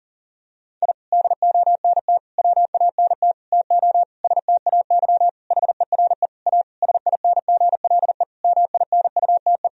Morse code, heard as a keyed tone, about 40 wpm, 700 Hz. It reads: I DONT WANT TO STAY HERE A SINGLE MINUTE
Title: Sentences from Top 1000 Words - Review Part 4 - 40wpm